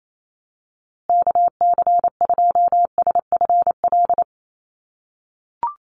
Morse code, heard as a keyed tone, about 28 wpm, 700 Hz. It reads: KC2SFL
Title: USA Call Signs Repeated 2x 28wpm